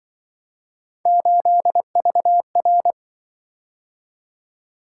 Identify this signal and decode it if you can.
Morse code, heard as a keyed tone, about 25 wpm, 700 Hz. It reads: 8VR
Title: Speed-Racing - Three Letter-Number Combinations - 20wpm